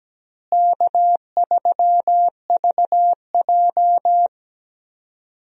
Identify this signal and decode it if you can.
Morse code, heard as a keyed tone, about 17 wpm, 700 Hz. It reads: K3VJ
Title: USA Call Signs - 4 Characters - Repeated 3x - 17wpm